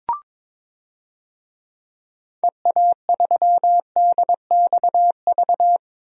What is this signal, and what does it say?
Morse code, heard as a keyed tone, about 21 wpm, 700 Hz. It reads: EA3DXV
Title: DXCC-100 Call Signs - 6 Characters - 22wpm